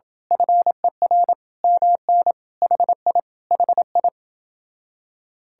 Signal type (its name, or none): Morse code